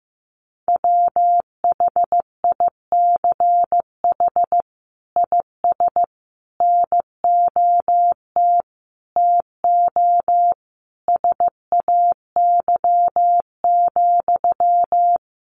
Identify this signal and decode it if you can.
Morse code, heard as a keyed tone, about 15 wpm, 700 Hz. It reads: WHICH IS NOT TO SAY,
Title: Sentences from Top 1400 Words 15wpm